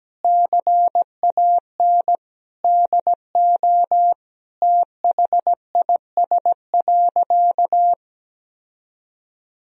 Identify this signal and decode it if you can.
Morse code, heard as a keyed tone, about 17 wpm, 700 Hz. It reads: CAN DO THIS.
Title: Sentences from Top 1000 Words - Review Part 1 - 17wpm